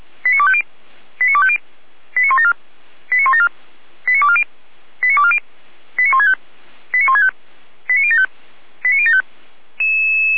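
Five-tone selective call (ZVEI), heard as quick runs of five tones, tones of about 70 ms. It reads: 89250 89250 89174 89174 89240 89240 89165 89165 89075 89075
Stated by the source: ZVEI five-tone call 89173 received by SDR, archived by fire brigade Feuerwehr Orlamünde.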